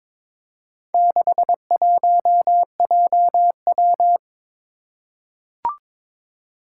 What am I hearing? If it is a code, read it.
Morse code, heard as a keyed tone, about 22 wpm, 700 Hz. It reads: 61JW
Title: Four Challenging Letter-Number Combinations 22wpm